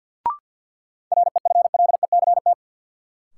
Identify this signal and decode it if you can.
Morse code, heard as a keyed tone, about 48 wpm, 700 Hz. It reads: REFLECT
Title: Sentences from Top 1500 Words - New Word Drill - 50wpm